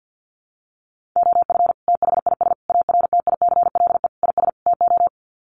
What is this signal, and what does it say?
Morse code, heard as a keyed tone, about 50 wpm, 700 Hz. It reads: OF THIS ARTICLE IS TO
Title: Sets of 5 Words Encore2 50wpm